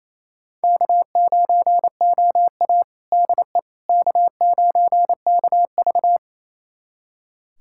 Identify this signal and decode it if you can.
Morse code, heard as a keyed tone, about 27 wpm, 700 Hz. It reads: K9OA DE K9KV
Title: USA Callsign DE Callsign - 4 Characters - 28wpm